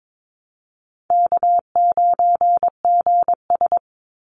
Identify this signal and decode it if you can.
Morse code, heard as a keyed tone, about 22 wpm, 700 Hz. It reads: K9GS
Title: Call Signs Repeated 2x - 22wpm